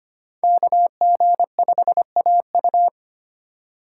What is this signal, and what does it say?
Morse code, heard as a keyed tone, about 25 wpm, 700 Hz. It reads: KG5AU
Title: USA Call Signs - 5 Characters - 3x Character Spacing - 25wpm